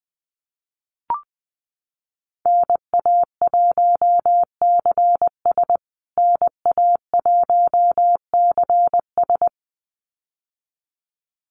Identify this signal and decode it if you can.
Morse code, heard as a keyed tone, about 20 wpm, 700 Hz. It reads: NA1CS NA1CS
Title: USA Call Signs Repeated 2x 20wpm